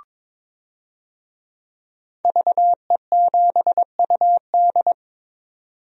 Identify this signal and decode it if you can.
Morse code, heard as a keyed tone, about 22 wpm, 700 Hz. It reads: VE7UD
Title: DXCC-100 Call Signs - 5 Characters - 22wpm